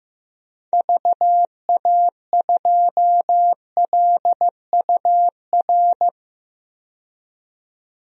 Morse code, heard as a keyed tone, about 15 wpm, 700 Hz. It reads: VA2LUR
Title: DXCC-100 Call Signs - 6 Characters - 15wpm